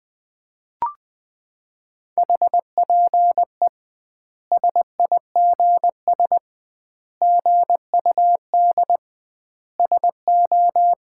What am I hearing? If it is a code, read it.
Morse code, heard as a keyed tone, about 20 wpm, 700 Hz. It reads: HPE SIGS GUD SO
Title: Common QSO Phrases - 2x Word-Spacing - 20wpm